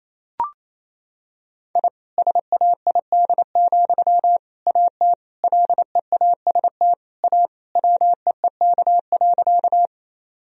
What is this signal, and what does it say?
Morse code, heard as a keyed tone, about 28 wpm, 700 Hz. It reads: I SAID, AT LEAST A WEEK.
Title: Sentences from Top 1000 Words - Review Part 2 - 28wpm